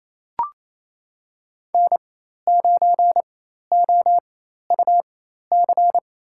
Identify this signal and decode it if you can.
Morse code, heard as a keyed tone, about 27 wpm, spaced as for 13 wpm, 700 Hz. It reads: N9OUC
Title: USA Call Signs - 4x Character Spacing - 28wpm